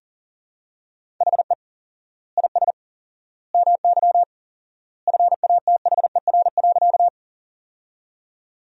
Morse code, heard as a keyed tone, about 37 wpm, 700 Hz. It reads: HE IS MY FATHER.
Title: 4x Word-Spacing - Sentences from Top 300 Words - 40wpm